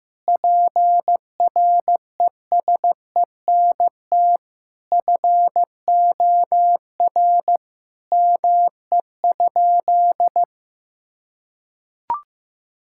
Morse code, heard as a keyed tone, about 15 wpm, 700 Hz. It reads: PRESENT FOR ME?
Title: Sentences from Top 1000 Words - Review Part 5 - 15wpm